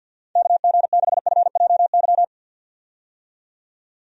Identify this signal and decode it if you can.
Morse code, heard as a keyed tone, about 45 wpm, 700 Hz. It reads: KG6FJY
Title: USA Call Signs - 6 Characters - 50wpm